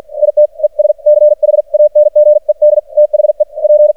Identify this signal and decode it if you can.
Morse code, heard as a keyed tone, about 52 wpm, spaced as for 45 wpm, 600 Hz. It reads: IT ES CHANGED THE P
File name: archived / CRSnet